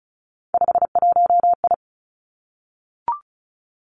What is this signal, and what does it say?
Morse code, heard as a keyed tone, about 36 wpm, 700 Hz. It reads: 51I
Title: Three Challenging Letter-Number Combinations 35wpm